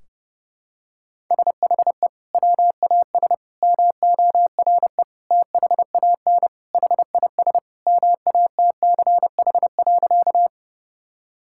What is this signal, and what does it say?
Morse code, heard as a keyed tone, about 30 wpm, 700 Hz. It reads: SHE WAS MORE THAN HIS MATCH.
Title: Sentences from Top 1600 Words 30wpm